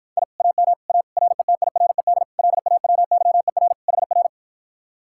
Morse code, heard as a keyed tone, about 52 wpm, 700 Hz. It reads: I AM A RETIRED LAWYER HR